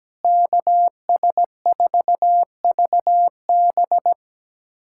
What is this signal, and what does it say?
Morse code, heard as a keyed tone, about 17 wpm, 700 Hz. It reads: KS4VB
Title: USA Callsign DE Callsign 17wpm